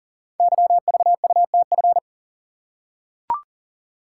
Morse code, heard as a keyed tone, about 42 wpm, 700 Hz. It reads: YVUTF
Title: Code Groups - 5 Letter:Numbers - 40wpm